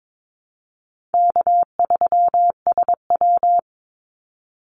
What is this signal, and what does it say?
Morse code, heard as a keyed tone, about 22 wpm, 700 Hz. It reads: K3SW
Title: Easy Call Signs - 22wpm